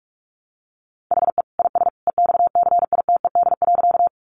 Morse code, heard as a keyed tone, about 44 wpm, 700 Hz. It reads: HE IS EXCITED.
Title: Additional Easy Phrases 45wpm